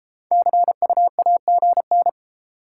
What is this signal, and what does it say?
Morse code, heard as a keyed tone, about 32 wpm, 700 Hz. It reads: CUAGN